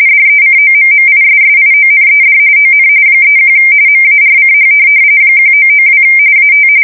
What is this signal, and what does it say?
RTTY, about 45.45 baud. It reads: DECOMMISSIONED ON 31 MARCH 1992 AT
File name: fsk170 / USS-Missouri